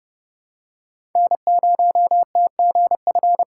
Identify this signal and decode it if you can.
Morse code, heard as a keyed tone, about 29 wpm, 700 Hz. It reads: N0TGF
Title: USA Call Signs - 5 Characters - 4x Character Spacing - 30wpm